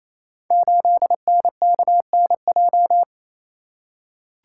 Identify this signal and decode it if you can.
Morse code, heard as a keyed tone, about 28 wpm, 700 Hz. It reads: 8NKNJ
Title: Code Groups - 5 Letter:Numbers - 28wpm